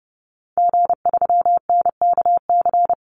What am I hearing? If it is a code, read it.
Morse code, heard as a keyed tone, about 30 wpm, 700 Hz. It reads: G3NKC
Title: Call Signs Repeated 3x - 30wpm